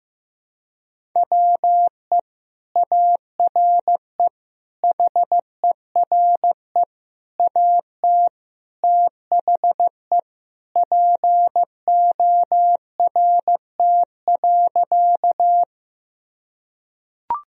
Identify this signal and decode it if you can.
Morse code, heard as a keyed tone, about 15 wpm, 700 Hz. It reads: WE ARE HERE AT THE PORT.
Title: Sentences from Top 1000 Words - Review Part 1 - 15wpm